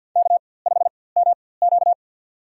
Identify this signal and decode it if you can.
Morse code, heard as a keyed tone, about 47 wpm, spaced as for 23 wpm, 700 Hz. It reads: K5MQ
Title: USA Call Signs - 4x Character Spacing - 50wpm